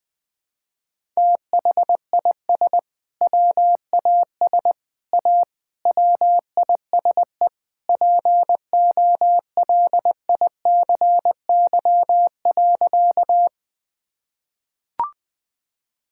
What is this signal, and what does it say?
Morse code, heard as a keyed tone, about 21 wpm, 700 Hz. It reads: THIS WAS A WISE POLICY.
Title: Sentences from Top 1600 Words 20wpm